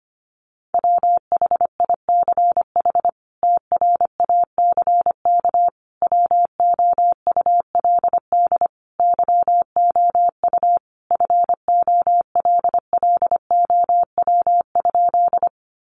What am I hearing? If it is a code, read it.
Morse code, heard as a keyed tone, about 25 wpm, 690 Hz. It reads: WHICH TRACK WOULD YOU FOLLOW?
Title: Sentences from Top 1100 Words 25wpm